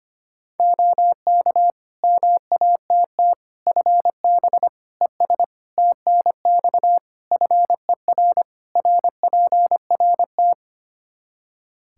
Morse code, heard as a keyed tone, about 25 wpm, 700 Hz. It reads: OK MATT FB ES TNX FER RPRT
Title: Practice QSO - Exchange 2 - Info - 25wpm